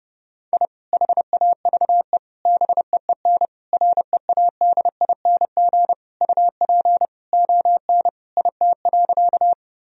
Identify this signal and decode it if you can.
Morse code, heard as a keyed tone, about 32 wpm, 700 Hz. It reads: I HAVE BEEN READING UP ON IT.
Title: Sentences from Top 1000 Words - Review Part 3 - 30wpm